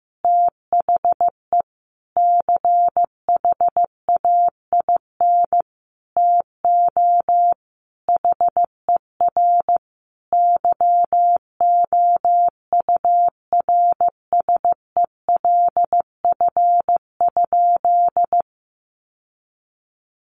Morse code, heard as a keyed tone, about 15 wpm, 700 Hz. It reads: THE CHAIN TO HER YOURSELF?
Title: Sentences from Top 1500 Words 15wpm